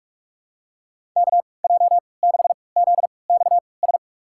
Morse code, heard as a keyed tone, about 46 wpm, spaced as for 26 wpm, 700 Hz. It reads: KJ6ZXS
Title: USA Call Signs - 6 Characters - 3x Character Spacing - 45wpm